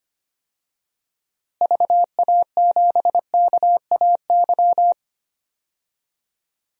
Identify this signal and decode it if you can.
Morse code, heard as a keyed tone, about 25 wpm, 700 Hz. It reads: VA7KAY
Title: DXCC-100 Call Signs - 6 Characters - Repeated 2x - 25wpm